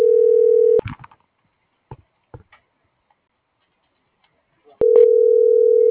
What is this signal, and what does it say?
Ringback tone (North America), 440 + 480 Hz.
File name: CENTRAL GENERAL / out laura.albarran